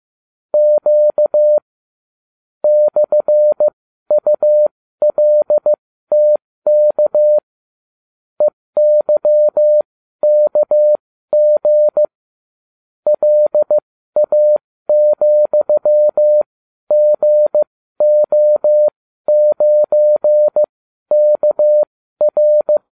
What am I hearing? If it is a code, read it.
Morse code, heard as a keyed tone, about 15 wpm, spaced as for 12 wpm, 600 Hz. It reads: Q /ULTK EYKG LA,GO9KR